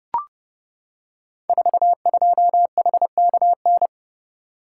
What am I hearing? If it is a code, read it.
Morse code, heard as a keyed tone, about 30 wpm, 700 Hz. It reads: 42HKN
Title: Code Groups - 5 Letter:Numbers - 30wpm